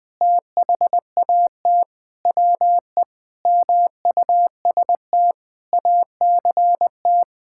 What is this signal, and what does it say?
Morse code, heard as a keyed tone, about 20 wpm, 700 Hz. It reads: THAT WE MUST ACT
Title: Sentences with 6-Word Phrases - 20wpm